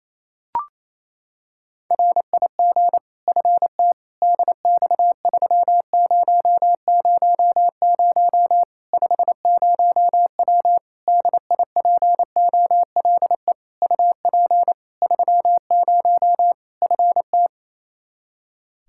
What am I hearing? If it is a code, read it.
Morse code, heard as a keyed tone, about 28 wpm, 700 Hz. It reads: RIG FT DX3000 50W DIPOLE UP 30 FT
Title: Practice QSO - Exchange 2 - Info - 28wpm